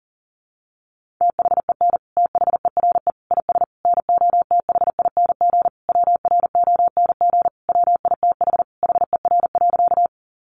Morse code, heard as a keyed tone, about 40 wpm, 700 Hz. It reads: THEN THERE IS NOTHING WRONG WITH HER.
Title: Sentences from Top 900 Words 40wpm